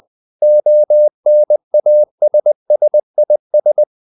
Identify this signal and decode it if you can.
Morse code, heard as a keyed tone, about 20 wpm, 600 Hz. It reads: ONASSIS
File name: CW / Practice / EnglishQuotations0320